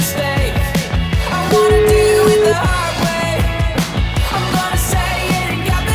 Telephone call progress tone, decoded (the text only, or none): ringback tone